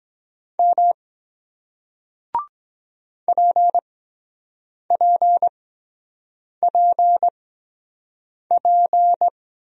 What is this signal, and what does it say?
Morse code, heard as a keyed tone, about 23 wpm, spaced as for 7 wpm, 700 Hz. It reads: M PPPP